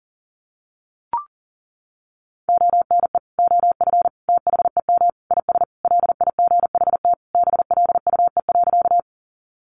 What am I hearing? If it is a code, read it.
Morse code, heard as a keyed tone, about 40 wpm, 700 Hz. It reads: ONE OF THEM IS LIGHT BLUE.
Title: Additional Easy Phrases 40wpm